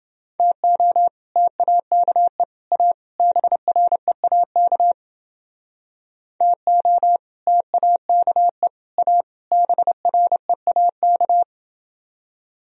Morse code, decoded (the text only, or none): TO TAKE A BREAK TO TAKE A BREAK